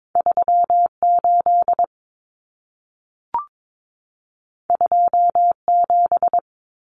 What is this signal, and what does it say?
Morse code, heard as a keyed tone, about 22 wpm, 700 Hz. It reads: 38 27
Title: Numbers - 2-Digit 22wpm